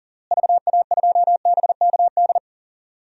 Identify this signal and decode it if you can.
Morse code, heard as a keyed tone, about 40 wpm, 700 Hz. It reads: VA2BKD
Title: DXCC-100 Call Signs - 6 Characters - Repeated 2x - 40wpm